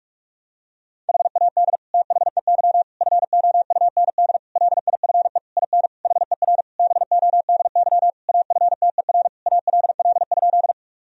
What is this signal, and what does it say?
Morse code, heard as a keyed tone, about 46 wpm, 700 Hz. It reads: HAD THEY FOUND LIFE IN HER BODY AFTER ALL?
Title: Sentences from Top 1000 Words - Review Part 2 - 45wpm